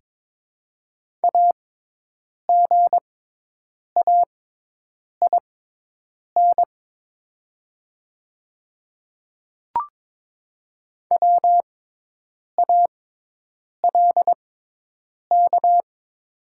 Morse code, heard as a keyed tone, about 22 wpm, spaced as for 8 wpm, 700 Hz. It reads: AGAIN WALK
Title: 6x Character Spacing - Top 300 Words - 22wpm